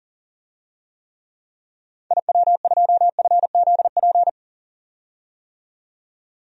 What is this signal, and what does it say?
Morse code, heard as a keyed tone, about 40 wpm, 700 Hz. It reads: IW2FZP
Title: DXCC-100 Call Signs - 6 Characters - 40wpm